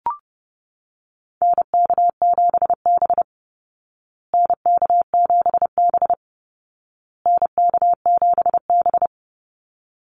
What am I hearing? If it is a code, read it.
Morse code, heard as a keyed tone, about 30 wpm, 700 Hz. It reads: NK7B NK7B NK7B
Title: Call Signs Repeated 2x - 30wpm